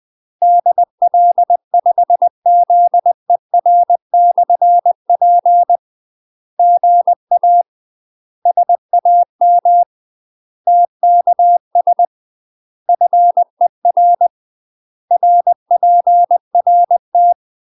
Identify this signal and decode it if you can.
Morse code, heard as a keyed tone, about 20 wpm, 700 Hz. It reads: DL5ZER/P GA SAM TKS FER RPRT
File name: UK MASTER QSOs TRAINING / 20WPM / UK MASTER QSOs 0066